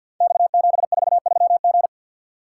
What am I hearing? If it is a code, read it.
Morse code, heard as a keyed tone, about 52 wpm, 700 Hz. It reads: X743G
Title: Code Groups - 5 Letter:Numbers - 50wpm